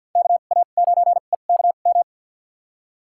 Morse code, heard as a keyed tone, about 52 wpm, spaced as for 37 wpm, 700 Hz. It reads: KA9EKM
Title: USA Call Signs - 2x Character Spacing - 50wpm